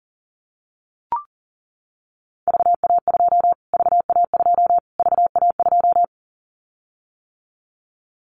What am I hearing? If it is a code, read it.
Morse code, heard as a keyed tone, about 40 wpm, 700 Hz. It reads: VA2 VA2 VA2
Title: Callsign Prefixes for Top 25 DXCC - Repeated 3x - 40wpm